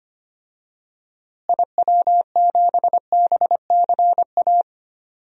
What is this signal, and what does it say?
Morse code, heard as a keyed tone, about 25 wpm, 700 Hz. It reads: IW7BCA
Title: DXCC-100 Call Signs - 6 Characters - Repeated 2x - 25wpm